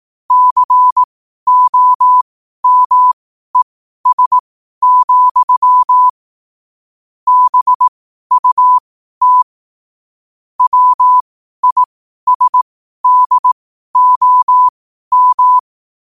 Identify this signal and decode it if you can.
Morse code, heard as a keyed tone, about 18 wpm, spaced as for 12 wpm, 1000 Hz. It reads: COMES, BUT WISDOM
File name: qotdfiles / 12wpm